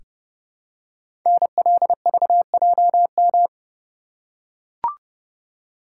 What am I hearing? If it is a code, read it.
Morse code, heard as a keyed tone, about 29 wpm, 700 Hz. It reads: NLVJM
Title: Code Groups - 5 Letters - 30wpm